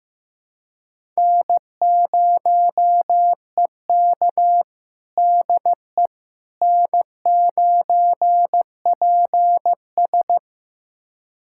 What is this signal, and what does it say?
Morse code, heard as a keyed tone, about 15 wpm, 700 Hz. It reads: N0EK DE N9PS
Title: USA Callsign DE Callsign - 4 Characters - 15wpm